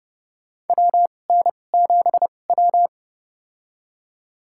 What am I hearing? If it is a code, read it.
Morse code, heard as a keyed tone, about 30 wpm, spaced as for 22 wpm, 700 Hz. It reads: WN7W
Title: USA Call Signs - 4 Characters - 2x Character Spacing - 30wpm